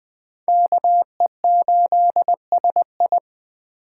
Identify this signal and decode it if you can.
Morse code, heard as a keyed tone, about 20 wpm, 700 Hz. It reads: KE8SI
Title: Speed Racing - USA 2x2 Call Signs 17wpm